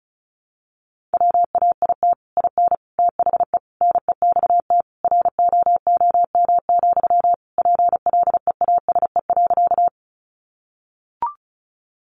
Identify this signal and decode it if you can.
Morse code, heard as a keyed tone, about 34 wpm, 700 Hz. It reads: WAIT IN THE NEXT ROOM, PLEASE.
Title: Sentences from Top 1500 Words 35wpm